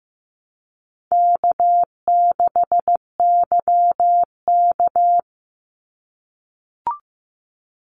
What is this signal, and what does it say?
Morse code, heard as a keyed tone, about 15 wpm, 700 Hz. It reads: K6YK
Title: NA SOTA Chaser Callsigns 15wpm